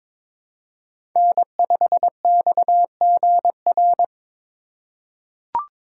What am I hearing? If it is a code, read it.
Morse code, heard as a keyed tone, about 22 wpm, 700 Hz. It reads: N5XGR
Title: USA Call Signs - 5 Characters - 3x Character Spacing - 22wpm